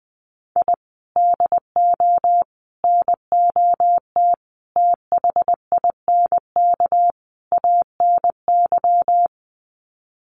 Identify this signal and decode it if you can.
Morse code, heard as a keyed tone, about 20 wpm, 700 Hz. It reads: I DO NOT THINK ANY
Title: Sets of 5 Words Encore 20wpm